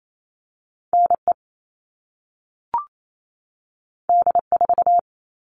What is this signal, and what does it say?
Morse code, heard as a keyed tone, about 28 wpm, 700 Hz. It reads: NE D4
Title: All Cut Numbers - 2-Digit 28wpm